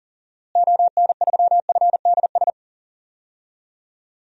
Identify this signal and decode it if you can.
Morse code, heard as a keyed tone, about 40 wpm, 700 Hz. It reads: ON3FDS
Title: DXCC-100 Call Signs - 6 Characters - Repeated 2x - 40wpm